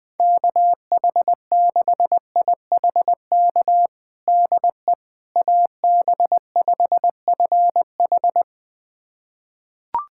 Morse code, heard as a keyed tone, about 20 wpm, 700 Hz. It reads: KH6IHK DE AB5FH
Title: USA Callsign DE Callsign 20wpm